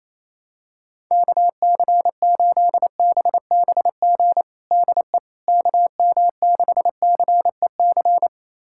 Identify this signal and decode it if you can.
Morse code, heard as a keyed tone, about 28 wpm, 700 Hz. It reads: KC8BBG DE KM6CEC
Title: USA Callsign DE Callsign 28wpm